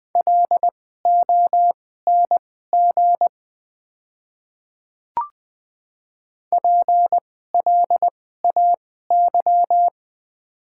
Morse code, heard as a keyed tone, about 20 wpm, spaced as for 15 wpm, 700 Hz. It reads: LONG PLAY
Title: 2x Character Spacing - Top 200 Words - 20wpm